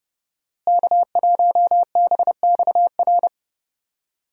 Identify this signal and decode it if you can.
Morse code, heard as a keyed tone, about 30 wpm, 700 Hz. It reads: K1BXR
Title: USA Call Signs - 5 Characters - Repeated 3x - 30wpm